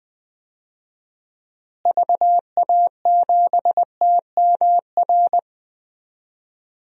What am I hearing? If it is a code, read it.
Morse code, heard as a keyed tone, about 20 wpm, 700 Hz. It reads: VA7TMR
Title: DXCC-100 Call Signs - 6 Characters - Repeated 3x - 20wpm